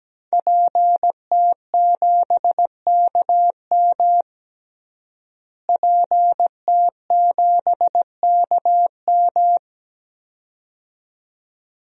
Morse code, heard as a keyed tone, about 17 wpm, 700 Hz. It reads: PT7KM PT7KM
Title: DXCC-100 Call Signs - 5 Characters - Repeated 2x - 17wpm